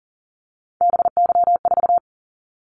Morse code, heard as a keyed tone, about 38 wpm, 700 Hz. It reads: BY4
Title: Callsign Prefixes for Top 100 DXCC - 40wpm